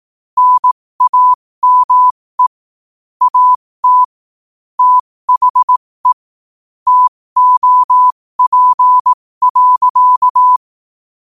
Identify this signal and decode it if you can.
Morse code, heard as a keyed tone, about 18 wpm, spaced as for 15 wpm, 1000 Hz. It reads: NAME AT THE TOP.
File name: qotdfiles / 15wpm